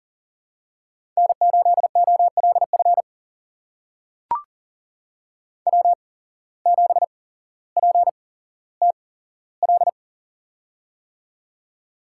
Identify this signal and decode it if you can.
Morse code, heard as a keyed tone, about 40 wpm, spaced as for 17 wpm, 700 Hz. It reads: N8OLF W7PTL
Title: USA Call Signs - 5 Characters - 8x Character Spacing - 40wpm